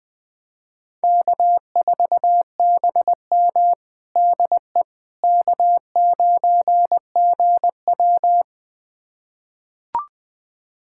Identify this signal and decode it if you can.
Morse code, heard as a keyed tone, about 20 wpm, 700 Hz. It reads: K4BM DE K9GW
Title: USA Callsign DE Callsign - 4 Characters - 20wpm